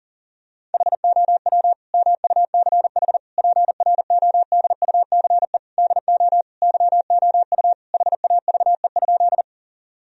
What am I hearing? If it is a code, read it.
Morse code, heard as a keyed tone, about 40 wpm, 700 Hz. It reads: HOW MUCH PRODUCE DO YOU HAVE?
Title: Sentences from Top 1000 Words - Review Part 3 - 40wpm